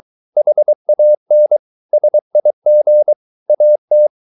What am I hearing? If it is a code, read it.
Morse code, heard as a keyed tone, about 23 wpm, 600 Hz. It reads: HAN SIG AT
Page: Ferie 23 wpm